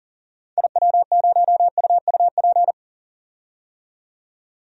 Morse code, heard as a keyed tone, about 37 wpm, 700 Hz. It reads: IW0UUP
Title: DXCC-100 Call Signs - 6 Characters - 40wpm